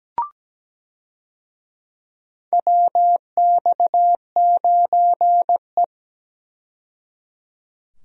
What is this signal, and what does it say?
Morse code, heard as a keyed tone, about 17 wpm, 700 Hz. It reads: WX9E
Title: USA Call Signs - 4 Characters - 17wpm